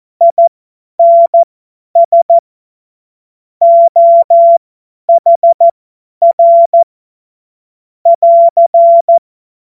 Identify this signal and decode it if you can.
Morse code, heard as a keyed tone, about 14 wpm, spaced as for 10 wpm, 680 Hz. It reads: INS OHR <AR>
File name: Morsekurs / Witze / Witz 12. Oktober 2021, 50 F70